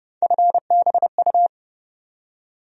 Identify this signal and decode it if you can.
Morse code, heard as a keyed tone, about 31 wpm, 700 Hz. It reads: FBU
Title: USA Call Sign Postfixes - 3 Letters 30wpm